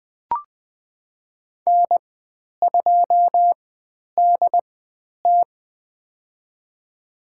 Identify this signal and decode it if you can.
Morse code, heard as a keyed tone, about 20 wpm, spaced as for 10 wpm, 700 Hz. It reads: N2DT
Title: 095 Lesson 16g - Letter c - Call Signs 20wpm